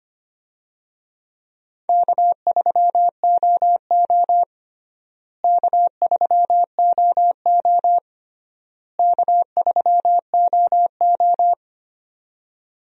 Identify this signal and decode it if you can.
Morse code, heard as a keyed tone, about 25 wpm, 700 Hz. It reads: K3OO K3OO K3OO
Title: USA Call Signs - 4 Characters - Repeated 3x - 25wpm